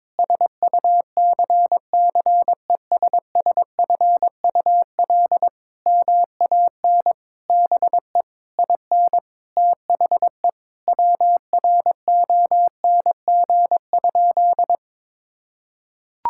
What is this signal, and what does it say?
Morse code, heard as a keyed tone, about 22 wpm, 700 Hz. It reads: SUCCESSFUL MAN BE IN THE WRONG?
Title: Sentences from Top 1000 Words - Review Part 6 - 22wpm